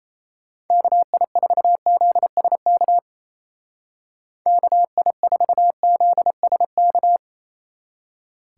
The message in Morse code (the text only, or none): KI4ZSK KI4ZSK